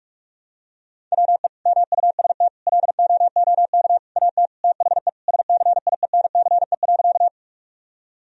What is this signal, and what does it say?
Morse code, heard as a keyed tone, about 44 wpm, 700 Hz. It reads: WE MUST LOOK AT THE SCIENCE.
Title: Sentences from Top 1000 Words - Review Part 2 - 45wpm